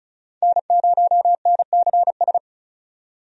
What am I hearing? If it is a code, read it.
Morse code, heard as a keyed tone, about 36 wpm, 700 Hz. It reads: N0NCS
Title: USA Call Signs - 5 Characters - 35wpm